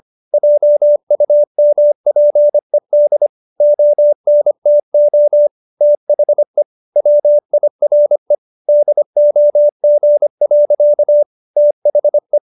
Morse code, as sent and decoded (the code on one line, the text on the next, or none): .--- ..- -- .--. -.. / --- -. - --- / - .... / .-- .. .-. / -.. --- --. .-.-.- / - .... .
JUMPED ONTO THE WIRE DOG. THE